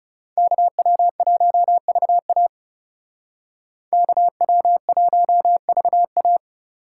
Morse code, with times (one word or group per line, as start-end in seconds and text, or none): 0.4-2.5 KW1VA
3.9-6.4 KW1VA